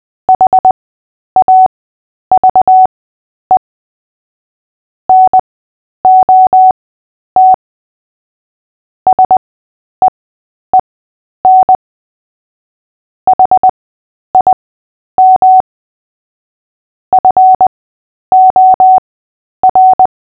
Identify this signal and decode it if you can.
Morse code, heard as a keyed tone, about 20 wpm, spaced as for 10 wpm, 750 Hz. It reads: HAVE NOT SEEN HIM FOR
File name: Short-Story-5-10wpm